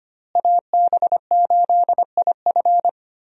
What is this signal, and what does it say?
Morse code, heard as a keyed tone, about 25 wpm, 700 Hz. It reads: AB8IF
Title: USA Call Signs - 5 Characters - Repeated 2x - 25wpm